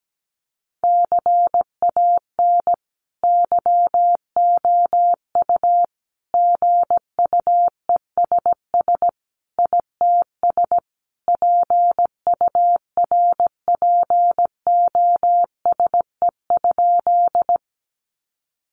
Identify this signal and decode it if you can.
Morse code, heard as a keyed tone, about 17 wpm, 700 Hz. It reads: CAN YOU GUESS ITS PURPOSE?
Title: Sentences from Top 1500 Words 17wpm